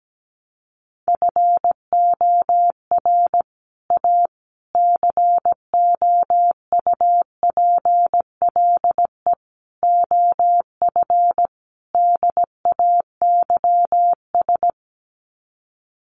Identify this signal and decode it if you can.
Morse code, heard as a keyed tone, about 17 wpm, 700 Hz. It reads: FOR A COUPLE OF DAYS
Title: Sets of 5 Words Encore2 17wpm